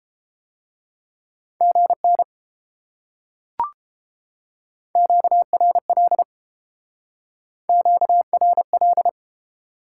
Morse code, heard as a keyed tone, about 30 wpm, 700 Hz. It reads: GN QRL QRL